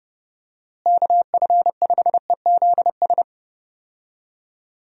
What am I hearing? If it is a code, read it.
Morse code, heard as a keyed tone, about 30 wpm, 700 Hz. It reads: KF5EZS
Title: USA Call Signs - 6 Characters - Repeated 3x - 30wpm